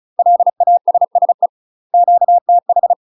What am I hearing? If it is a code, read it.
Morse code, heard as a keyed tone, about 34 wpm, 700 Hz. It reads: LASSE QTH
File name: UK MASTER QSOs TRAINING / 35WPM / UK MASTER QSOs 0000